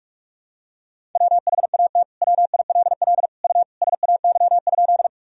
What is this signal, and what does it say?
Morse code, heard as a keyed tone, about 42 wpm, 700 Hz. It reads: WHAT WILL U SAY?
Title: Sentences from Top 200 Words and Abbreviations 45wpm